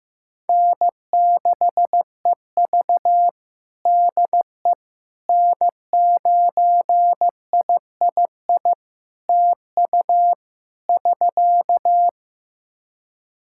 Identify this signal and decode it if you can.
Morse code, heard as a keyed tone, about 15 wpm, 700 Hz. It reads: N6EV DE N9III TU <SK>
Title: Practice QSO - Exchange 4 - Ending - 15wpm